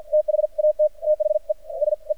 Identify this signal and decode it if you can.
Morse code, heard as a keyed tone, about 48 wpm, spaced as for 37 wpm, 610 Hz. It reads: THAT THESE
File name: archived / CRSnet